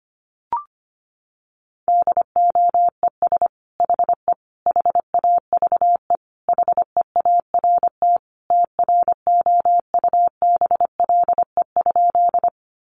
Morse code, heard as a keyed tone, about 25 wpm, 700 Hz. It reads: DOES HE HAVE HEART TROUBLE?
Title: Sentences from Top 1200 Words 25wpm